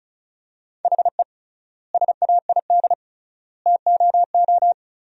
Morse code, heard as a keyed tone, about 34 wpm, 700 Hz. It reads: HE SAID TOO
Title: 3x Word-Spacing - Sentences from Top 200 Words - 35wpm